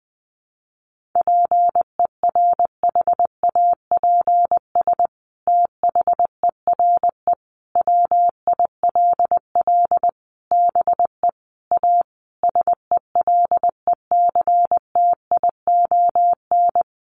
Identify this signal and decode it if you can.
Morse code, heard as a keyed tone, about 20 wpm, 700 Hz. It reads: PERHAPS THERE WILL BE A SELECTION
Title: Sentences from Top 1400 Words 20wpm